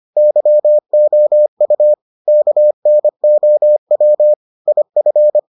Morse code, heard as a keyed tone, about 25 wpm, 600 Hz. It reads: YOU KNOW IF